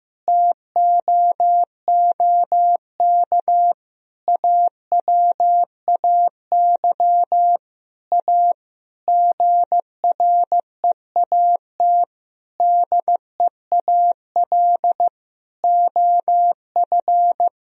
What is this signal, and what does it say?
Morse code, heard as a keyed tone, about 15 wpm, 700 Hz. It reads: TOOK AWAY A GREAT DEAL OF
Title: Sentences from Top 1000 Words - Review Part 1 - 15wpm